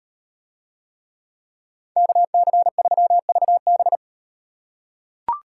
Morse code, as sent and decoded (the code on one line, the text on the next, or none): -.- -.-. ...-- ...- -...
KC3VB